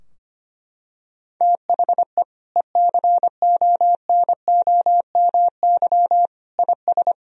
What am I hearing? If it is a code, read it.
Morse code, heard as a keyed tone, about 25 wpm, 700 Hz. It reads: THE ECONOMY IS
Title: Sentences from Top 1000 Words - Review Part 3 - 25wpm